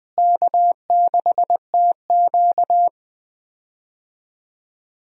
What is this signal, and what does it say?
Morse code, heard as a keyed tone, about 20 wpm, 700 Hz. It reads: K6TQ
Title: USA Call Signs - 4 Characters - Repeated 3x - 20wpm